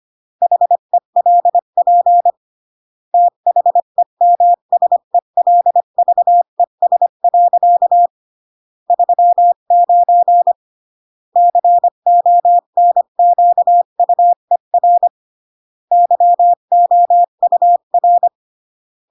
Morse code, heard as a keyed tone, about 25 wpm, 700 Hz. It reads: HELP THEMSELVES. 39 CONQUER YOUR
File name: CW / jodorowsky / 82 commandments 25-20